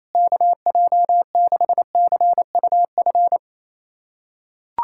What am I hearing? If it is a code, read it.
Morse code, heard as a keyed tone, about 28 wpm, 700 Hz. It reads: KJ6CUF
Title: USA Callsign DE Callsign - 6 Characters - 28wpm